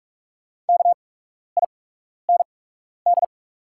Morse code, heard as a keyed tone, about 44 wpm, spaced as for 12 wpm, 700 Hz. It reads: KIND
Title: 8x Character Spacing - Top 300 Words - 45wpm